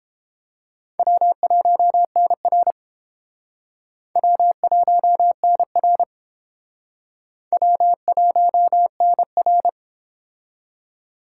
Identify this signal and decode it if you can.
Morse code, heard as a keyed tone, about 29 wpm, 700 Hz. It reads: W1NR W1NR W1NR